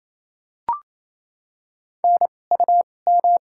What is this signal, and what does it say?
Morse code, heard as a keyed tone, about 28 wpm, spaced as for 20 wpm, 700 Hz. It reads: NUM
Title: 2x Character Spacing - Top 500 Words - 28wpm